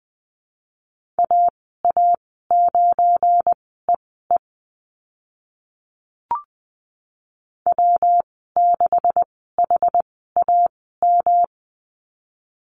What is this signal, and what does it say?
Morse code, heard as a keyed tone, about 20 wpm, spaced as for 15 wpm, 700 Hz. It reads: AA9EE W6HAM
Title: 179 Lesson 29g - Slash - Call Signs 20wpm